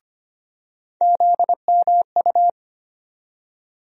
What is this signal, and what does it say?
Morse code, heard as a keyed tone, about 25 wpm, 700 Hz. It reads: ZMU
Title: USA Call Sign Postfixes - 3 Letters 25wpm